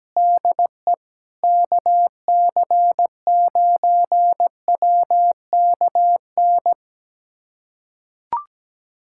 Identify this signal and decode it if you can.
Morse code, heard as a keyed tone, about 17 wpm, 700 Hz. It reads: DE KC9WKN
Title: USA Callsign DE Callsign 17wpm